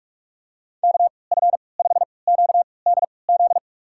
Morse code, heard as a keyed tone, about 47 wpm, spaced as for 26 wpm, 700 Hz. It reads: KF5QDZ
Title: USA Call Signs - 6 Characters - 3x Character Spacing - 45wpm